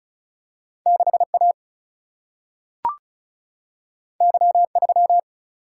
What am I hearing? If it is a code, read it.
Morse code, heard as a keyed tone, about 36 wpm, 700 Hz. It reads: 6A Y3
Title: Single-Letter Single-Number 35wpm